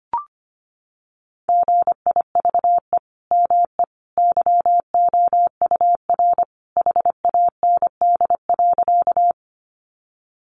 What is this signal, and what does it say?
Morse code, heard as a keyed tone, about 25 wpm, 700 Hz. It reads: GIVE ME YOUR HAND.
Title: Sentences from Top 200 Words 25wpm